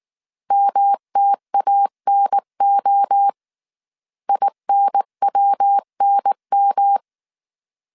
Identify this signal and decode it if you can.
Morse code, heard as a keyed tone, about 19 wpm, 800 Hz. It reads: MTANO INWNM
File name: CW / DET / 1 3 1500C S B 18wpm 1916